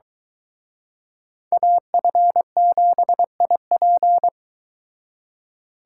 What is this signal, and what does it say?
Morse code, heard as a keyed tone, about 24 wpm, 700 Hz. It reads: AF7IP